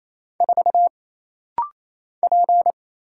Morse code, heard as a keyed tone, about 28 wpm, 700 Hz. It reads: 4 P
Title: Single Letter-Number Alt - Rapid-Fire 28wpm